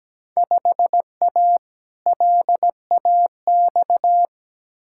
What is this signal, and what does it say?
Morse code, heard as a keyed tone, about 17 wpm, 700 Hz. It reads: 5A LAX
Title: Field Day Exchanges - 17wpm